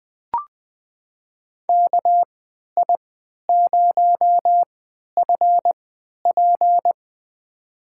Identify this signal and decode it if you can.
Morse code, heard as a keyed tone, about 20 wpm, spaced as for 11 wpm, 700 Hz. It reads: KI0FP
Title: USA Call Signs - 5 Characters - 3x Character Spacing - 20wpm